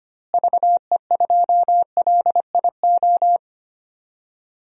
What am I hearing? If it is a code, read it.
Morse code, heard as a keyed tone, about 25 wpm, 700 Hz. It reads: VE2LIO
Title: DXCC-100 Call Signs - 6 Characters - Repeated 2x - 25wpm